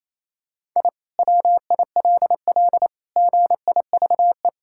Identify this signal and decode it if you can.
Morse code, heard as a keyed tone, about 28 wpm, 700 Hz. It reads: I WILL GIVE
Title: Sentences from Top 1000 Words - Review Part 6 - 28wpm